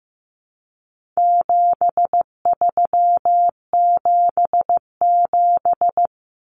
Morse code, heard as a keyed tone, about 15 wpm, 700 Hz. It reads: 7377
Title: Common Cut Numbers - 4-Digit 15wpm